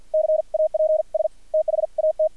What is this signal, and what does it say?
Morse code, heard as a keyed tone, about 50 wpm, spaced as for 43 wpm, 630 Hz, with radio noise. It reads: CAPS THAT
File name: archived / CRSnet